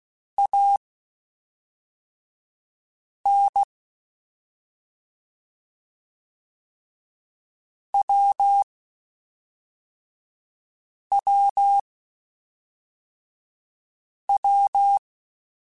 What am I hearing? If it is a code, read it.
Morse code, heard as a keyed tone, about 16 wpm, spaced as for 3 wpm, 800 Hz. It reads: AN WWW